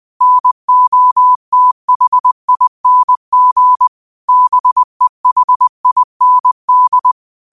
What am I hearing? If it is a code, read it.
Morse code, heard as a keyed tone, about 20 wpm, 1000 Hz. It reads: NOTHING BEHIND
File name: qotdfiles / 20wpm